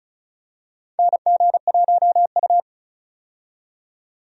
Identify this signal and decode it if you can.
Morse code, heard as a keyed tone, about 36 wpm, 700 Hz. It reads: NG1U
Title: USA Call Signs - 4 Characters - 35wpm